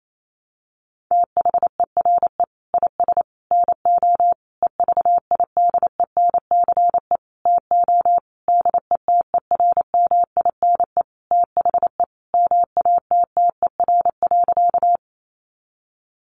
Morse code, heard as a keyed tone, about 28 wpm, 700 Hz. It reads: THERE IS NO EVIDENCE TO DETERMINE THE MATTER.
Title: Sentences from Top 1300 Words 28wpm